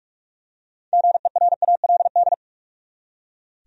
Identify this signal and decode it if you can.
Morse code, heard as a keyed tone, about 44 wpm, 700 Hz. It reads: GERALD
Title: USA Common Names - 45wpm